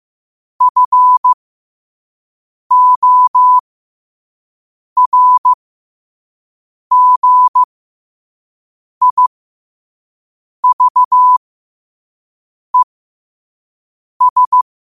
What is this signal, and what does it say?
Morse code, heard as a keyed tone, about 15 wpm, spaced as for 5 wpm, 1000 Hz. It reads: FORGIVES
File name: qotdfiles / 5wpm